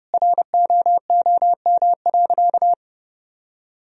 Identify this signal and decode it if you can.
Morse code, heard as a keyed tone, about 30 wpm, 700 Hz. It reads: ROOM.
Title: Sentences from Top 1000 Words - Review Part 1 - 30wpm